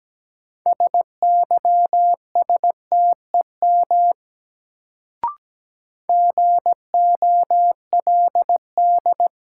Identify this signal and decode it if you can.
Morse code, heard as a keyed tone, about 17 wpm, 700 Hz. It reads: SYSTEM GOLD
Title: Sentences from Top 1000 Words - Review Part 3 - New Word Drill - 17wpm